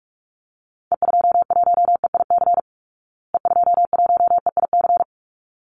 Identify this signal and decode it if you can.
Morse code, heard as a keyed tone, about 44 wpm, 700 Hz. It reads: E21EIC E21EIC
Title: Call Signs Repeated 3x - 45wpm